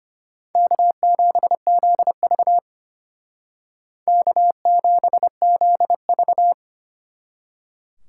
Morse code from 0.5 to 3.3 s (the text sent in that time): K7ZV